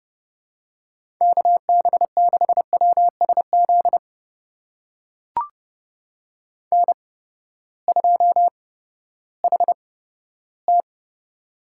Morse code, heard as a keyed tone, about 30 wpm, spaced as for 15 wpm, 700 Hz. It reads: KB6WSZ N2HT
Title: USA Call Signs - 8x Character Spacing - 30wpm